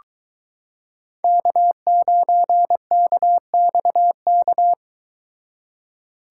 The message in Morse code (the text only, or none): K9KXK